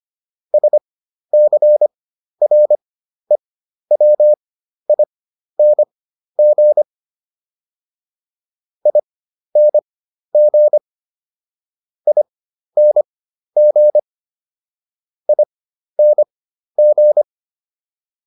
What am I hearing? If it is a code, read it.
Morse code, heard as a keyed tone, about 25 wpm, spaced as for 12 wpm, 600 Hz. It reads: SCREWING ING ING ING